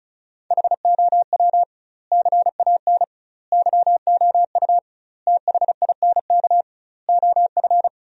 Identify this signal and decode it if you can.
Morse code, heard as a keyed tone, about 36 wpm, 700 Hz. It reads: HOW CAN YOU THINK OF
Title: 2x Word-Spacing - Sentences from Top 200 Words - 35wpm